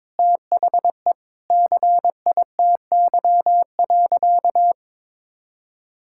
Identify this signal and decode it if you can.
Morse code, heard as a keyed tone, about 22 wpm, 700 Hz. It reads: THE CITY.
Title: Sentences from Top 1000 Words - Review Part 6 - 22wpm